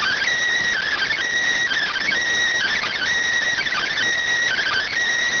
Emergency Alert System (SAME) tones, heard as harsh buzzing bursts. It is heard from 0.8 to 4.9 s.
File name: findings / axial iodine 3mm 205.gen-12 rx